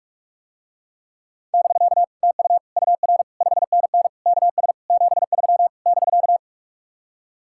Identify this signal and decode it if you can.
Morse code, heard as a keyed tone, about 49 wpm, 700 Hz. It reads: <BK> TU UR 5NN KS 73 <BK>